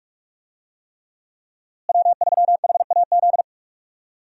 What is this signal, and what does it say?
Morse code, heard as a keyed tone, about 42 wpm, 700 Hz. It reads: W3HAZ
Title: USA Call Signs - 5 Characters - Repeated 3x - 45wpm